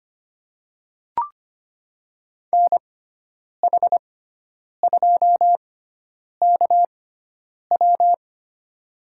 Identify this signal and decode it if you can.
Morse code, heard as a keyed tone, about 25 wpm, spaced as for 9 wpm, 700 Hz. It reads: NH2KW
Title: USA Call Signs - 5 Characters - 6x Character Spacing - 25wpm